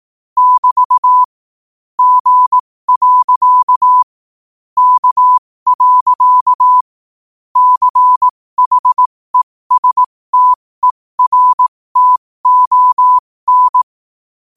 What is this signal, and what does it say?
Morse code, heard as a keyed tone, about 18 wpm, spaced as for 15 wpm, 1000 Hz. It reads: <BT> G. K. CHESTERTON